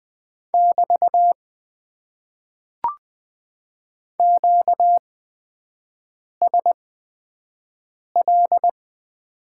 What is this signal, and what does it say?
Morse code, heard as a keyed tone, about 20 wpm, spaced as for 5 wpm, 700 Hz. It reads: <BT> QSL
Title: Top 90 QSO Elements - 8x Character Spacing - 20wpm